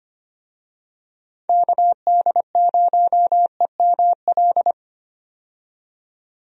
Morse code, heard as a keyed tone, about 25 wpm, 700 Hz. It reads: KD0EML
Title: USA Call Signs - 6 Characters - 25wpm